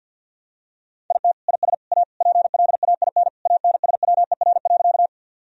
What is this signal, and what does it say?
Morse code, heard as a keyed tone, about 52 wpm, 700 Hz. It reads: IT IS A PLAIN ANSWER.
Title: Sentences from Top 1000 Words - Review Part 2 - 50wpm